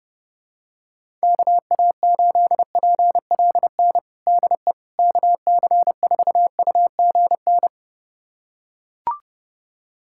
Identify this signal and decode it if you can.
Morse code, heard as a keyed tone, about 30 wpm, 690 Hz. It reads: KA8PLN DE KC4UGN
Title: USA Callsign DE Callsign 30wpm